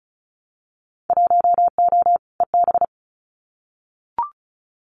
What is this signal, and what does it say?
Morse code, heard as a keyed tone, about 34 wpm, 700 Hz. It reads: 1O EB
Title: Winter Field Day Exchanges 35wpm